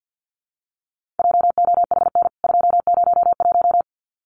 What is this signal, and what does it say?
Morse code, heard as a keyed tone, about 50 wpm, 700 Hz. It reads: JOHN 201
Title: CWT Exchange 50wpm